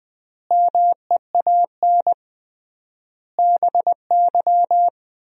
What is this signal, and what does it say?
Morse code, heard as a keyed tone, about 20 wpm, 700 Hz. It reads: MEAN BY
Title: 3x Word-Spacing - Sentences from Top 200 Words - 20wpm